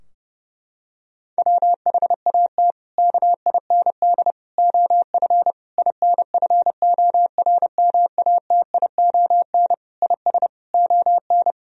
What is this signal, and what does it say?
Morse code, heard as a keyed tone, about 32 wpm, 700 Hz. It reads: WHAT KIND OF INFORMATION IS ON
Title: Sentences from Top 1000 Words - Review Part 3 - 30wpm